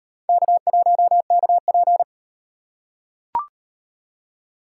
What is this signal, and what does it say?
Morse code, heard as a keyed tone, about 37 wpm, 700 Hz. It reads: K1KP